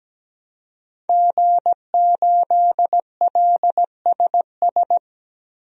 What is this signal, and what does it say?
Morse code, heard as a keyed tone, about 17 wpm, 700 Hz. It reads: G8LSS
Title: DXCC-100 Call Signs - 5 Characters - 17wpm